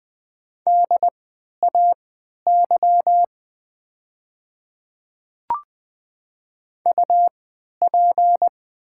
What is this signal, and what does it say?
Morse code, heard as a keyed tone, about 20 wpm, spaced as for 11 wpm, 700 Hz. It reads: DAY UP
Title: 3x Character Spacing - Top 100 Words - 20wpm